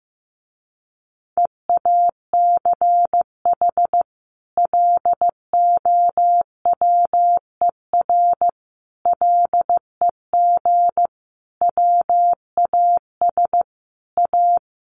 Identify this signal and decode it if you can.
Morse code, heard as a keyed tone, about 15 wpm, 700 Hz. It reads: EACH LOWER LEG WAS A
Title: A Sound Of Thunder 15wpm